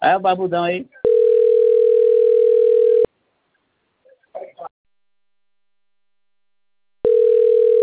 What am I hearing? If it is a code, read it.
Ringback tone (North America), 440 + 480 Hz, with speech.
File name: rg-1600-unknown- - - .42950